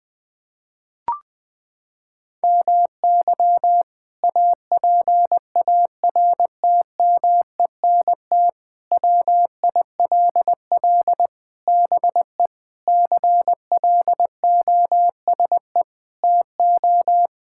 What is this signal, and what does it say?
Morse code, heard as a keyed tone, about 20 wpm, 700 Hz. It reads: MY APARTMENT WILL BE CLOSE TO
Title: Sentences from Top 1000 Words - Review Part 6 - 20wpm